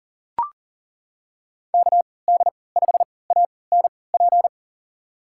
Morse code, heard as a keyed tone, about 40 wpm, spaced as for 23 wpm, 700 Hz. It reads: KD5ANP
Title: USA Call Signs - 3x Character Spacing - 40wpm